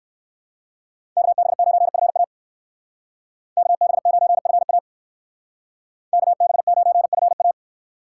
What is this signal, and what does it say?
Morse code, heard as a keyed tone, about 67 wpm, 700 Hz. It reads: KB9FA KB9FA KB9FA